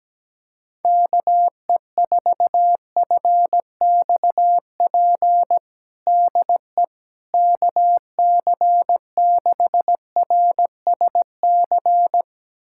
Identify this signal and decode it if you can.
Morse code, heard as a keyed tone, about 17 wpm, 700 Hz. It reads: KE4FXP DE KC6RSC
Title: USA Callsign DE Callsign - 6 Characters - 17wpm